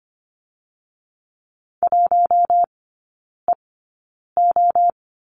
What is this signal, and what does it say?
Morse code, heard as a keyed tone, about 25 wpm, spaced as for 9 wpm, 700 Hz. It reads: 1EO
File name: Lesson 17e - Letter u - Three Characters 25wpm